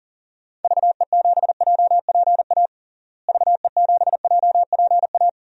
Morse code, heard as a keyed tone, about 39 wpm, 700 Hz. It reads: VE7JPA VE7JPA
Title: DXCC-100 Call Signs - 6 Characters - Repeated 2x - 40wpm